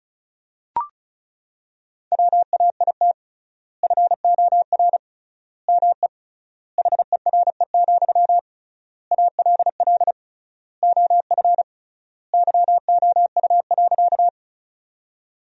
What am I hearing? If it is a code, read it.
Morse code, heard as a keyed tone, about 34 wpm, 700 Hz. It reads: WAIT FOR ME HERE, ALL OF YOU.
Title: 3x Word-Spacing - Sentences from Top 500 Words - 35wpm